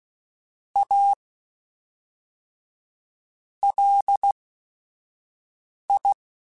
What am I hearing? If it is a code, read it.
Morse code, heard as a keyed tone, about 16 wpm, 800 Hz. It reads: A L I
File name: QSOs / QSOs-mp3s / WD0BGP 2@5